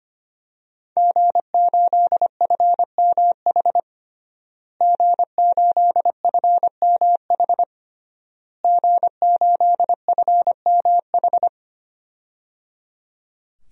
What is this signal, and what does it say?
Morse code, heard as a keyed tone, about 25 wpm, 700 Hz. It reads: G8FMH G8FMH G8FMH
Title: DXCC-100 Call Signs - 5 Characters - Repeated 3x - 25wpm